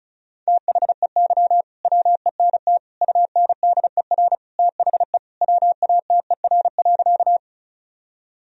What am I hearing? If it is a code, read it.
Morse code, heard as a keyed tone, about 36 wpm, 700 Hz. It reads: THEY WENT UNDER THE WATER.
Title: Sentences from Top 1000 Words - Review Part 1 - 35wpm